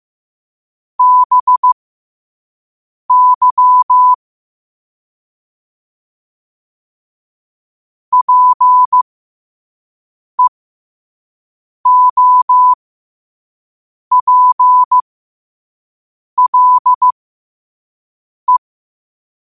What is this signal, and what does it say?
Morse code, heard as a keyed tone, about 15 wpm, spaced as for 5 wpm, 1000 Hz. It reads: BY PEOPLE
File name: qotdfiles / 5wpm